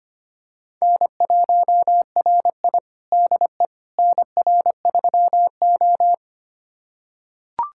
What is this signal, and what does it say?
Morse code, heard as a keyed tone, about 25 wpm, 700 Hz. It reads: N1RI DE NR3O
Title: USA Callsign DE Callsign - 4 Characters - 25wpm